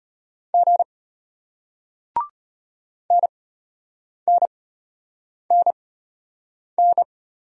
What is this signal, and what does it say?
Morse code, heard as a keyed tone, about 34 wpm, spaced as for 8 wpm, 700 Hz. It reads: G NNNN